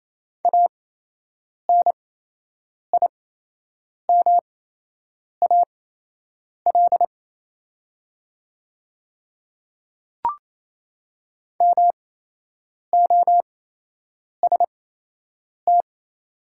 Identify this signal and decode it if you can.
Morse code, heard as a keyed tone, about 28 wpm, spaced as for 8 wpm, 700 Hz. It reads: ANIMAL MOST
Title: 8x Character Spacing - Top 200 Words - 28wpm